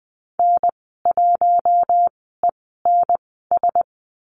Morse code, heard as a keyed tone, about 20 wpm, spaced as for 15 wpm, 700 Hz. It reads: N1ENS
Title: 128 Lesson 21g - Number 6 - Call Signs 20wpm